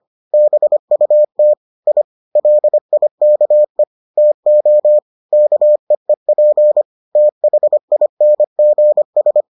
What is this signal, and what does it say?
Morse code, heard as a keyed tone, about 25 wpm, 600 Hz. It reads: BUT I LIKE TO KEEP THINGS